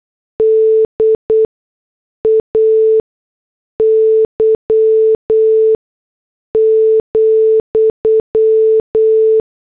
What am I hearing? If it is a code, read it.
Morse code, heard as a keyed tone, about 8 wpm, spaced as for 6 wpm, 440 Hz. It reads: DAY,